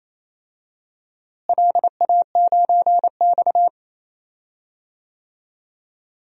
Morse code, heard as a keyed tone, about 28 wpm, 700 Hz. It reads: LA9X
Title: DXCC-100 Call Signs - 4 Characters - Repeated 2x - 28wpm